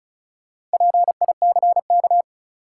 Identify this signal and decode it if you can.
Morse code, heard as a keyed tone, about 36 wpm, 700 Hz. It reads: PICK
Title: Sentences from Top 1000 Words - New Word Drill - 35wpm